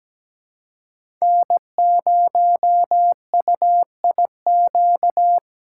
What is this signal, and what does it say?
Morse code, heard as a keyed tone, about 17 wpm, 700 Hz. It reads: N0UIQ
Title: USA Call Signs - 5 Characters - 3x Character Spacing - 17wpm